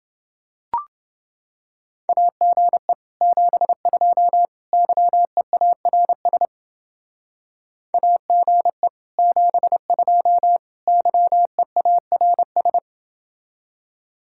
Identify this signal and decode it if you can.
Morse code, heard as a keyed tone, about 30 wpm, 700 Hz. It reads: AGE 72 YEARS AGE 72 YEARS